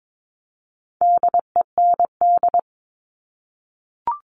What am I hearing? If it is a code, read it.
Morse code, heard as a keyed tone, about 22 wpm, 700 Hz. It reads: DEND
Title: All Cut Numbers - 4-Digit 22wpm